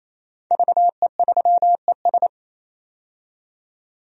Morse code, heard as a keyed tone, about 27 wpm, 700 Hz. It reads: VE3ES
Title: DXCC-100 Call Signs - 5 Characters - 28wpm